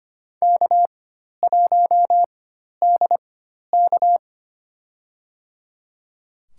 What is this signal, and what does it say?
Morse code, heard as a keyed tone, about 25 wpm, spaced as for 12 wpm, 700 Hz. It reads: K1DK
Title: USA Call Signs - 4 Characters - 4x Character Spacing - 25wpm